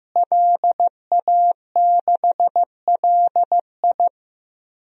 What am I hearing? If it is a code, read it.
Morse code, heard as a keyed tone, about 15 wpm, 700 Hz. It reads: LA6LI
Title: DXCC-100 Call Signs - 5 Characters - Repeated 2x - 15wpm